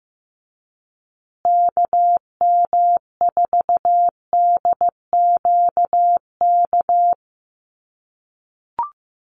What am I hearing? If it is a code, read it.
Morse code, heard as a keyed tone, about 15 wpm, 700 Hz. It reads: KM4DQK
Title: USA Call Signs Repeated 2x 15wpm